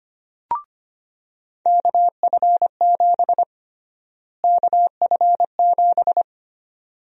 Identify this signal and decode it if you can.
Morse code, heard as a keyed tone, about 25 wpm, 700 Hz. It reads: KF7 KF7
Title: USA Call Sign Prefixes - Repeated 2x - 25wpm